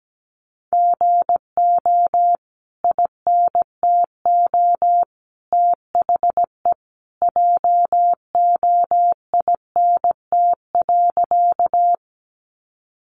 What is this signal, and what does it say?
Morse code, heard as a keyed tone, about 17 wpm, 700 Hz. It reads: GO INTO THE JOINT.
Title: Sentences from Top 1300 Words 17wpm